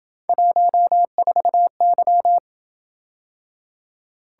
Morse code, heard as a keyed tone, about 28 wpm, 700 Hz. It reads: 14Y